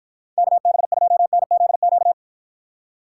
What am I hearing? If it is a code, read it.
Morse code, heard as a keyed tone, about 52 wpm, 700 Hz. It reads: KB2NZQ